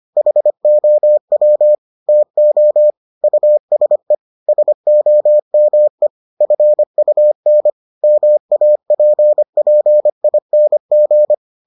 Morse code, heard as a keyed tone, about 25 wpm, 600 Hz. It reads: HOW TO USE SOME FUN MAPPING